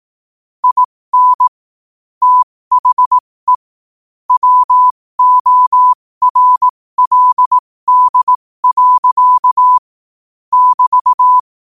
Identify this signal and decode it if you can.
Morse code, heard as a keyed tone, about 18 wpm, spaced as for 15 wpm, 1000 Hz. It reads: IN THE WORLD. <BT>
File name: qotdfiles / 15wpm